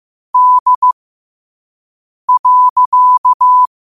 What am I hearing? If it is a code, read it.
Morse code, heard as a keyed tone, about 15 wpm, spaced as for 5 wpm, 1000 Hz. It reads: D.
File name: qotdfiles / 5wpm